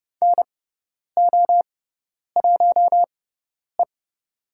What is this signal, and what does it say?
Morse code, heard as a keyed tone, about 30 wpm, spaced as for 10 wpm, 700 Hz. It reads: NO1E
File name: Lesson 17g - Letter u - Call Signs 30wpm